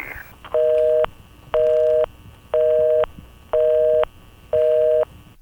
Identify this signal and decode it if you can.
Busy tone (North America), 480 + 620 Hz, with noise.